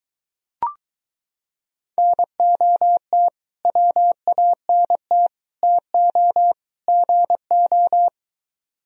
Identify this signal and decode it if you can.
Morse code, heard as a keyed tone, about 23 wpm, 700 Hz. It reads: NOT WANT TO GO